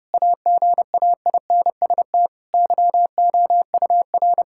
Morse code, heard as a keyed tone, about 31 wpm, 700 Hz. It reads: AGAINST YOUR
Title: Sentences from Top 1000 Words - Review Part 5 - 30wpm